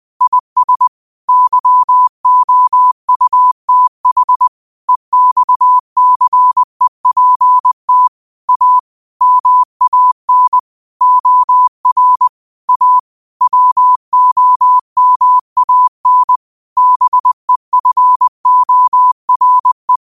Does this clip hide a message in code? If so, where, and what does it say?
Yes: Morse code from 0.2 to 20.0 s, heard as a keyed tone, about 20 wpm, 1000 Hz. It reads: IS YOUTH EXCEPT A MAN OR A WOMAN BEFORE